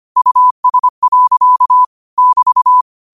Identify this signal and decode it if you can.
Morse code, heard as a keyed tone, about 25 wpm, 1000 Hz. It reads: US. <BT>
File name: qotdfiles / 25wpm